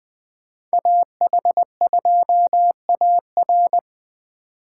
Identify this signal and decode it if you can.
Morse code, heard as a keyed tone, about 20 wpm, 700 Hz. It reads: AH2AR
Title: Speed Racing - USA 2x2 Call Signs 17wpm